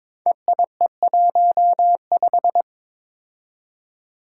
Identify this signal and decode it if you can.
Morse code, heard as a keyed tone, about 22 wpm, 700 Hz. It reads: EIE15
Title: Five Challenging Letter-Number Combinations 22wpm